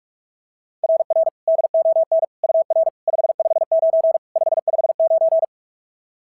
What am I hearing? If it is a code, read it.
Morse code, heard as a keyed tone, about 42 wpm, 660 Hz. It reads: RR DON UR 559 559
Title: SOTA QSO Practice 45wpm